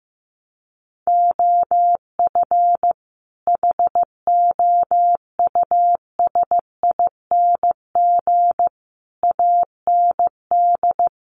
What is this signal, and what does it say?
Morse code, heard as a keyed tone, about 15 wpm, 700 Hz. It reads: OF HOUSING AND
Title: Sets of 5 Words Encore2 15wpm